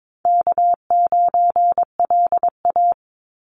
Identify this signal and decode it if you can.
Morse code, heard as a keyed tone, about 22 wpm, 700 Hz. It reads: K9LA
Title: Easy Call Signs Repeated 3x - 22wpm